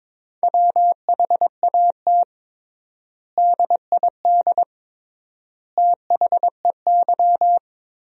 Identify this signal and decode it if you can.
Morse code, heard as a keyed tone, about 22 wpm, 700 Hz. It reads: WHAT DID THEY
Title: 3x Word-Spacing - Sentences from Top 100 Words - 22wpm